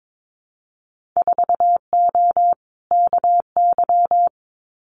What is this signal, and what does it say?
Morse code, heard as a keyed tone, about 22 wpm, 700 Hz. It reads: 4O KY
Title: Winter Field Day Exchanges Encore 22wpm